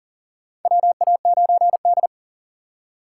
Morse code, heard as a keyed tone, about 40 wpm, 700 Hz. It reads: WA9D
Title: USA Call Signs - 3x Character Spacing - 40wpm